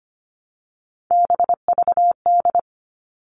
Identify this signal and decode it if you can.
Morse code, heard as a keyed tone, about 25 wpm, 700 Hz. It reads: BVD
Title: All Cut Numbers - 3-Digit 25wpm